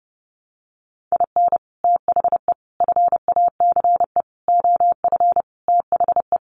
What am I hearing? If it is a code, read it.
Morse code, heard as a keyed tone, about 30 wpm, 700 Hz. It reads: IN THE FACE OF THE
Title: Sets of 5 Words 30wpm